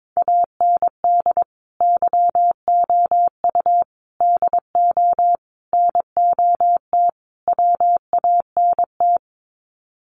Morse code, heard as a keyed tone, about 22 wpm, 700 Hz. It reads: AND YOU DO NOT WANT
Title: Sets of 5 Words Encore2 22wpm